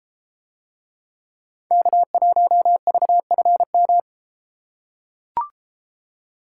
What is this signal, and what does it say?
Morse code, heard as a keyed tone, about 33 wpm, 700 Hz. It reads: K1VFM